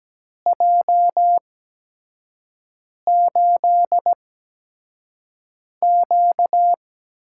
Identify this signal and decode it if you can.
Morse code, heard as a keyed tone, about 17 wpm, spaced as for 5 wpm, 700 Hz. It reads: J8Q
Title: USA Call Signs - 4 Characters - 8x Character Spacing - 17wpm